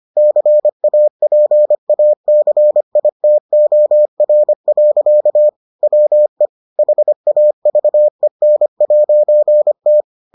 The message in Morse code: CAPACITOR. WE HAVEN'T

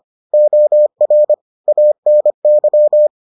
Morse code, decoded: OR ANY

